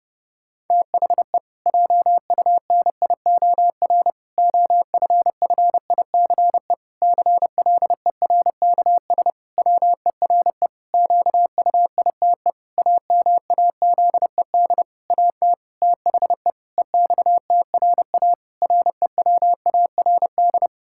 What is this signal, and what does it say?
Morse code, heard as a keyed tone, about 32 wpm, 700 Hz. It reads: THE JUNIOR OFFICE CLERKS WERE QUITE AMAZED AT THE EXTRA REWARD
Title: Pangram Challenge v2 - 30wpm